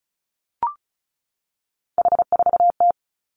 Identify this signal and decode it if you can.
Morse code, heard as a keyed tone, about 36 wpm, 700 Hz. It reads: H4T
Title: Three Challenging Letter-Number Combinations 35wpm